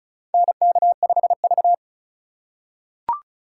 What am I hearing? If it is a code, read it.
Morse code, heard as a keyed tone, about 36 wpm, 690 Hz. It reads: NK5V
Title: USA Call Signs - 4 Characters - 2x Character Spacing - 35wpm